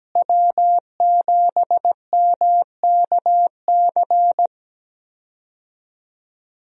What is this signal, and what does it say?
Morse code, heard as a keyed tone, about 17 wpm, 700 Hz. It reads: W7MKC
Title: USA Call Signs - 5 Characters - Repeated 3x - 17wpm